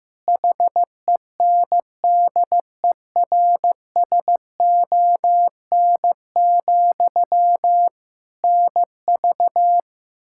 Morse code, heard as a keyed tone, about 15 wpm, 700 Hz. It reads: HENDERSON, NV
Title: Practice QSO - Exchange 1 - Beginning - 15wpm